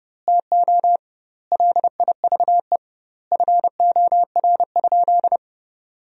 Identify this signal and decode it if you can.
Morse code, heard as a keyed tone, about 30 wpm, 700 Hz. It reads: TO LIVE FOR?
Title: 2x Word-Spacing - Sentences from Top 200 Words - 30wpm